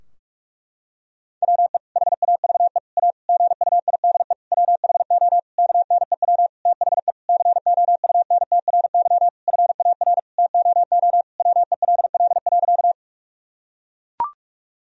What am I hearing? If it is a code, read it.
Morse code, heard as a keyed tone, about 42 wpm, 700 Hz. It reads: WE HAVE A GUIDE WHO KNEW THE COUNTRY FAR TOO WELL.
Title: Sentences from Top 1000 Words - Review Part 3 - 45wpm